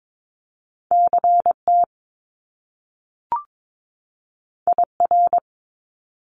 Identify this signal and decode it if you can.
Morse code, heard as a keyed tone, about 22 wpm, 700 Hz. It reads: CT IR
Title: Top Two Letter Combinations in Words 22wpm